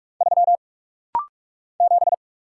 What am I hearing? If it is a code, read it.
Morse code, heard as a keyed tone, about 44 wpm, 690 Hz. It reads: 3 7
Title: Single Letter-Number - Rapid-Fire 45wpm